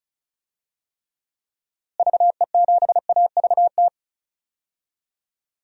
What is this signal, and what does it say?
Morse code, heard as a keyed tone, about 34 wpm, 700 Hz. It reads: VE7AVT
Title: DXCC-100 Call Signs - 6 Characters - Repeated 3x - 35wpm